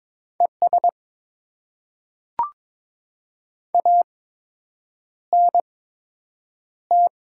Morse code, heard as a keyed tone, about 22 wpm, spaced as for 8 wpm, 700 Hz. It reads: ES ANT
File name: Top 20 QSO Elements - 8x Character Spacing - 22wpm